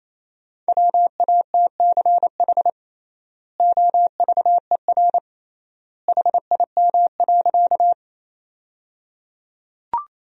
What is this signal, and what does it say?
Morse code, heard as a keyed tone, about 27 wpm, 700 Hz. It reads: WATCH OVER HIM.
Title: 3x Word-Spacing - Sentences from Top 300 Words - 28wpm